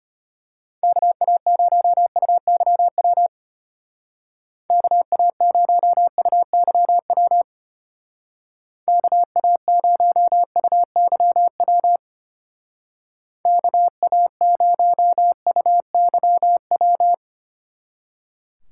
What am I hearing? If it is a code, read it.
Morse code, heard as a keyed tone, about 31 wpm, 700 Hz. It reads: KA0UYW KA0UYW KA0UYW KA0UYW